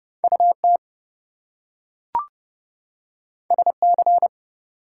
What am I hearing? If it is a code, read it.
Morse code, heard as a keyed tone, about 30 wpm, 700 Hz. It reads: UT SC
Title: US States and Canadian Provinces 30wpm